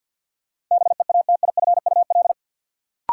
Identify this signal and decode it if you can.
Morse code, heard as a keyed tone, about 48 wpm, 700 Hz. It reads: BEATIFUL
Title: Sentences from Top 1300 Words - New Word Drill Alt - 50wpm